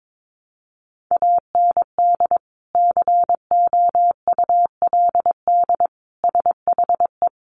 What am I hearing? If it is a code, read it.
Morse code, heard as a keyed tone, about 22 wpm, 700 Hz. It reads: AND COULD SHE
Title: Sentences from Top 100 Words 22wpm